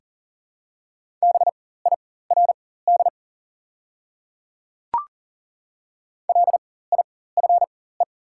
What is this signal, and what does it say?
Morse code, heard as a keyed tone, about 40 wpm, spaced as for 19 wpm, 700 Hz. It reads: BIRD LIFE
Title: 4x Character Spacing - Top 500 Words - 40wpm